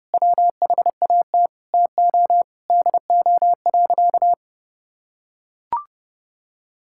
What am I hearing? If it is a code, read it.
Morse code, heard as a keyed tone, about 30 wpm, 700 Hz. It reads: WHAT TO DO.
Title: Sentences from Top 1000 Words - Review Part 5 - 30wpm